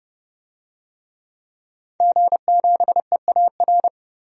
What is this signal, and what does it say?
Morse code, heard as a keyed tone, about 30 wpm, 700 Hz. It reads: G7EAR
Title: DXCC-100 Call Signs - 5 Characters - 30wpm